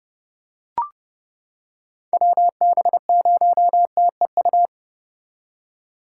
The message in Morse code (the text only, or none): WB0TEU